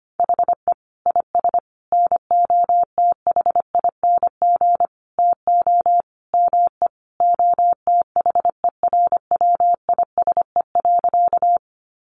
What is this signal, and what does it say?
Morse code, heard as a keyed tone, about 25 wpm, 700 Hz. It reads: HE IS NOTHING TO ME OTHERWISE.
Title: Sentences from Top 1200 Words 25wpm